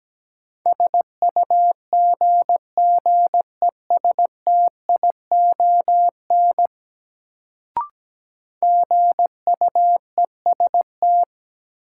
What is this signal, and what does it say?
Morse code, heard as a keyed tone, about 17 wpm, 700 Hz. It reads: SUGGESTION GUEST
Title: Sentences from Top 1600 Words - New Word Drill - 17wpm